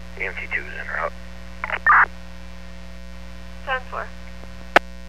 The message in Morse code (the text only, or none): none